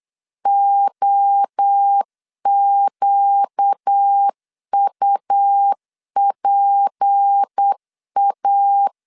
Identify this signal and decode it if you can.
Morse code, heard as a keyed tone, about 8 wpm, 800 Hz. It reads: OQUPA
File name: LEZIONI / 8 YJ45 / 8 Y 8wpm 3245